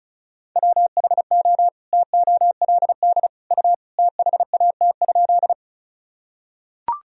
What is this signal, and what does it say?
Morse code, heard as a keyed tone, about 34 wpm, 700 Hz. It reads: WHO TOLD U THAT?
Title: Sentences from Top 400 Words and Abbreviations 35wpm